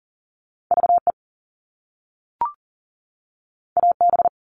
Morse code, heard as a keyed tone, about 40 wpm, 700 Hz. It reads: VE AB
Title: All Cut Numbers - 2-Digit 40wpm